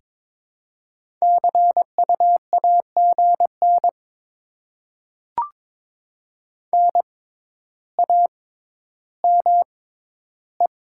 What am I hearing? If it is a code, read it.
Morse code, heard as a keyed tone, about 21 wpm, spaced as for 11 wpm, 700 Hz. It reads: CU AGN NAME
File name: Top 20 QSO Elements - 6x Character Spacing - 22wpm